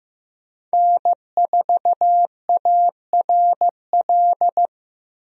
Morse code, heard as a keyed tone, about 15 wpm, 700 Hz. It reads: N4ARL
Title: USA Call Signs - 5 Characters - Repeated 2x - 15wpm